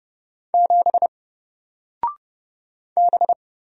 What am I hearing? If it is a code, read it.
Morse code, heard as a keyed tone, about 30 wpm, 700 Hz. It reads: 7 B
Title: Single Letter-Number 30wpm